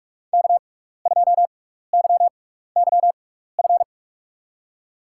Morse code, heard as a keyed tone, about 46 wpm, spaced as for 16 wpm, 700 Hz. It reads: K2YYF
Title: USA Call Signs - 5 Characters - 6x Character Spacing - 45wpm